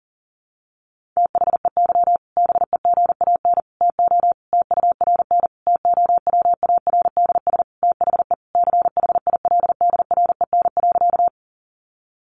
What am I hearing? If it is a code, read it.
Morse code, heard as a keyed tone, about 40 wpm, 700 Hz. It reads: THEY BEGAN TO TURN TOWARDS THE CHILDREN.
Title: Sentences from Top 1100 Words 40wpm